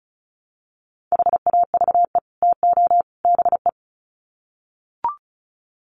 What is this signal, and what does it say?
Morse code, heard as a keyed tone, about 34 wpm, 700 Hz. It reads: HAVE TO BE
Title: Sets of 3 Words 35wpm